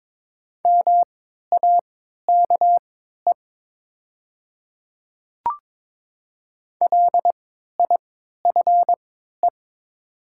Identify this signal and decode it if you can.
Morse code, heard as a keyed tone, about 22 wpm, spaced as for 13 wpm, 700 Hz. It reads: MAKE LIFE
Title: 3x Character Spacing - Top 300 Words - 22wpm